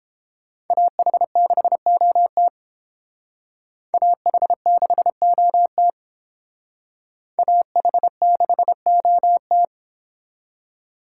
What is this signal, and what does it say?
Morse code, heard as a keyed tone, about 29 wpm, 700 Hz. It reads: AH6OT AH6OT AH6OT